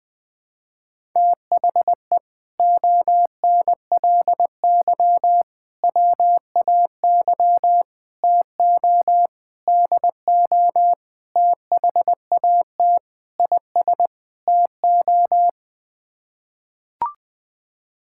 Morse code, heard as a keyed tone, about 20 wpm, 700 Hz. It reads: THE ONLY WAY TO DO THAT IS TO
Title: Sets of 8 Words - 20wpm